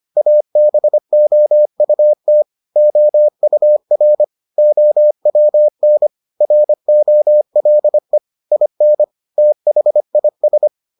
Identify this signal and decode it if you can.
Morse code, heard as a keyed tone, about 25 wpm, 600 Hz. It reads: ABOUT OUR OWN ROLE IN THIS